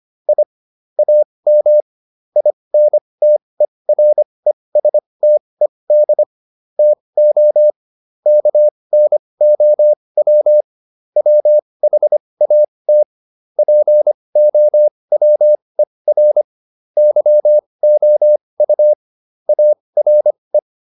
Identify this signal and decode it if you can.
Morse code, heard as a keyed tone, about 25 wpm, spaced as for 20 wpm, 600 Hz. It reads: I AM INTERESTED TO KNOW WHAT POWER YOU ARE